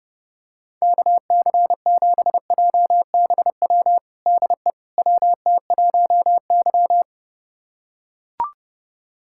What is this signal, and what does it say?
Morse code, heard as a keyed tone, about 32 wpm, 690 Hz. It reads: KC7JBW DE WT1Y
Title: USA Callsign DE Callsign 30wpm